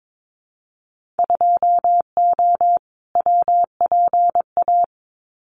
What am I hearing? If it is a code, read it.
Morse code, heard as a keyed tone, about 22 wpm, 700 Hz. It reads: 2O WPA
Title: Winter Field Day Exchanges 22wpm